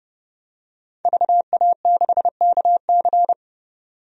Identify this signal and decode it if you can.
Morse code, heard as a keyed tone, about 32 wpm, 700 Hz. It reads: VA6KC
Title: DXCC-100 Call Signs - 5 Characters - 30wpm